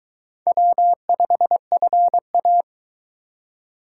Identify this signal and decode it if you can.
Morse code, heard as a keyed tone, about 23 wpm, 700 Hz. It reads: W5FA